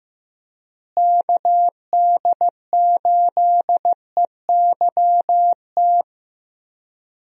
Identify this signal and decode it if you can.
Morse code, heard as a keyed tone, about 15 wpm, 700 Hz. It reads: KD8EYT
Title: USA Call Signs - 6 Characters - Repeated 3x - 15wpm